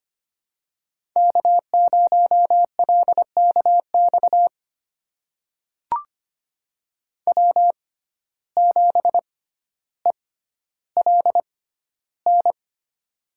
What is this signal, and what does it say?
Morse code, heard as a keyed tone, about 25 wpm, spaced as for 13 wpm, 700 Hz. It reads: K0LKX W7ELN
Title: USA Call Signs - 5 Characters - 6x Character Spacing - 25wpm